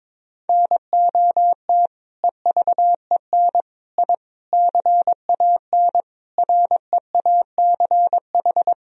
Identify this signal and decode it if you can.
Morse code, heard as a keyed tone, about 21 wpm, 700 Hz. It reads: NOT EVEN I CAN REACH
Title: Sentences from Top 1000 Words - Review Part 2 - 22wpm